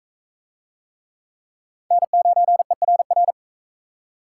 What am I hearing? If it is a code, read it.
Morse code, heard as a keyed tone, about 41 wpm, 700 Hz. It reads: N9ERR